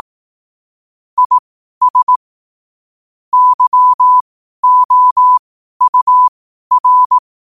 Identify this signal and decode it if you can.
Morse code, heard as a keyed tone, about 18 wpm, spaced as for 12 wpm, 1000 Hz. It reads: IS YOUR
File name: qotdfiles / 12wpm